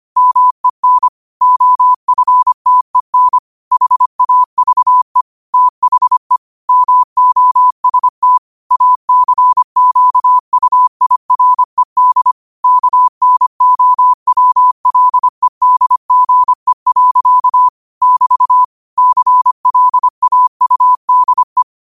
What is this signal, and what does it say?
Morse code, heard as a keyed tone, about 25 wpm, 1000 Hz. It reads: MEN OFTEN HAVE THE MOST ACQUIRED KNOWLEDGE. <BT> CLAUDE
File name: qotdfiles / 25wpm